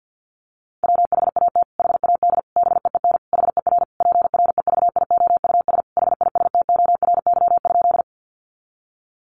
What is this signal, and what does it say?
Morse code, heard as a keyed tone, about 50 wpm, 700 Hz. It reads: WHAT HAD BEEN HER PREVIOUS HISTORY?
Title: Sentences from Top 1500 Words 50wpm